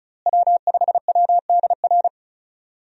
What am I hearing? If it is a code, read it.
Morse code, heard as a keyed tone, about 34 wpm, 700 Hz. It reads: W5WDR
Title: USA Call Signs - 5 Characters - Repeated 2x - 35wpm